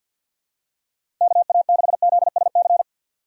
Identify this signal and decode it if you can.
Morse code, heard as a keyed tone, about 52 wpm, 700 Hz. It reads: KA6ZSC
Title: USA Call Signs - 6 Characters - 6x Character Spacing - 50wpm